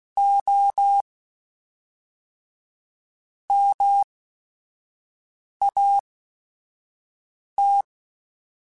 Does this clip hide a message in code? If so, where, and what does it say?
Yes: Morse code from 0.2 to 7.8 s, heard as a keyed tone, about 16 wpm, spaced as for 4 wpm, 800 Hz. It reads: OMAT